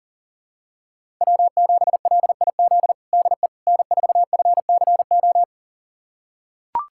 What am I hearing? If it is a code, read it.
Morse code, heard as a keyed tone, about 40 wpm, 700 Hz. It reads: W7LIZ DE N4FCO
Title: USA Callsign DE Callsign - 5 Characters - 40wpm